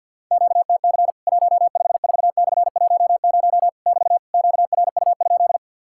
Morse code, heard as a keyed tone, about 50 wpm, 700 Hz. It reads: QTC 154/10 <BT> QRU?
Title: Worked All Europe DX Contest - QTC Exchange with Repeats - 50wpm